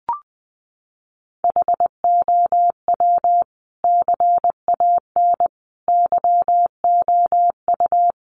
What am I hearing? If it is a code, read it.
Morse code, heard as a keyed tone, about 20 wpm, 700 Hz. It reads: HOW CAN YOU
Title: Sentences from Top 1300 Words 20wpm